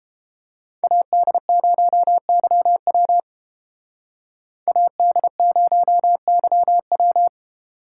Morse code, heard as a keyed tone, about 30 wpm, 700 Hz. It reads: AD0YW AD0YW